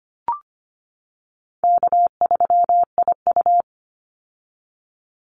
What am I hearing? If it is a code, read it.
Morse code, heard as a keyed tone, about 25 wpm, 700 Hz. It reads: K3IU
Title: Call Signs - 25wpm